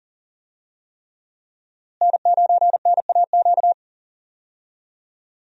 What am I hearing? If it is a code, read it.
Morse code, heard as a keyed tone, about 40 wpm, 700 Hz. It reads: N9NAQ
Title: USA Call Signs - 5 Characters - Repeated 3x - 40wpm